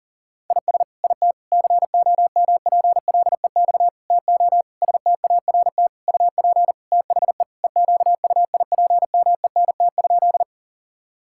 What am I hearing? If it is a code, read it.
Morse code, heard as a keyed tone, about 40 wpm, 700 Hz. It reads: IS IT COMPLEX TO START UP THE EQUIPMENT?
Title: Sentences from Top 1000 Words - Review Part 6 - 40wpm